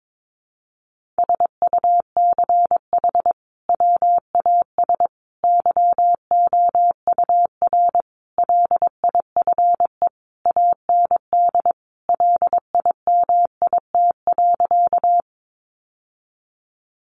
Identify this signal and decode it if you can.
Morse code, heard as a keyed tone, about 22 wpm, 700 Hz. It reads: SUCH WAS YOUR LIFE AND LIMIT.
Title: Sentences from Top 1500 Words 22wpm